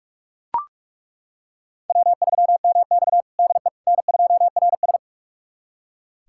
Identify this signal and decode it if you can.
Morse code, heard as a keyed tone, about 42 wpm, 700 Hz. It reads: W3MX DE N2RS
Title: USA Callsign DE Callsign - 4 Characters - 45wpm